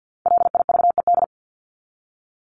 Morse code, heard as a keyed tone, about 50 wpm, 700 Hz. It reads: LIVED
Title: ED Suffix - Top 100 Words with Base Words 50wpm